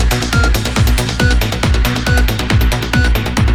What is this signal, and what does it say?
There is no code.